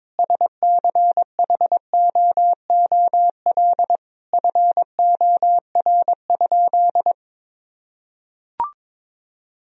Morse code, heard as a keyed tone, about 22 wpm, 700 Hz. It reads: SCHOOL FOR?
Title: Sentences from Top 1000 Words - Review Part 1 - 22wpm